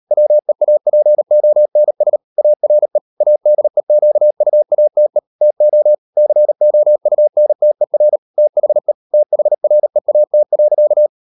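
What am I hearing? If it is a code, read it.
Morse code, heard as a keyed tone, about 38 wpm, 600 Hz. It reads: WEAPONS ARE ADEQUATE TO COUNTER THE THREAT.
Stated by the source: Morse code practice at 35–39 wpm sending English text of Ken Follett's novel Never, from CW med Gnister.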